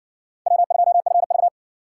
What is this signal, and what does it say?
Morse code, heard as a keyed tone, about 54 wpm, 700 Hz. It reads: W2FV